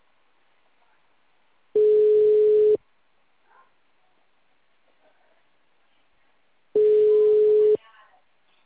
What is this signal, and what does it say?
Ringback tone (Europe, 425 Hz), with speech.
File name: out- -2015- - - .26642